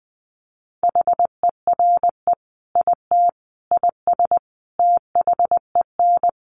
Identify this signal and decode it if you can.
Morse code, heard as a keyed tone, about 20 wpm, 700 Hz. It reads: HERE IT IS THEN
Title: Sentences from Top 1500 Words 20wpm